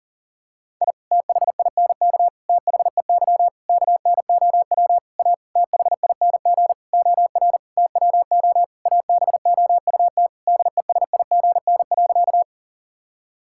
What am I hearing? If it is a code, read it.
Morse code, heard as a keyed tone, about 40 wpm, 700 Hz. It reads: I THINK THEY KNOW A THING OR TWO ABOUT DESIGN.
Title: Sentences from Top 1000 Words - Review Part 5 - 40wpm